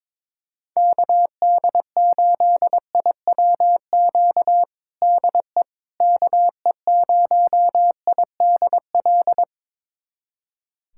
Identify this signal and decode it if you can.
Morse code, heard as a keyed tone, about 22 wpm, 700 Hz. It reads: KD8IWQ DE KE0IDL
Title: USA Callsign DE Callsign - 6 Characters - 22wpm